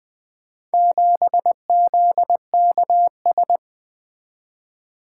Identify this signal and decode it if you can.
Morse code, heard as a keyed tone, about 20 wpm, 700 Hz. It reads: 7ZKS
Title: Four Letter-Number Combinations - 20wpm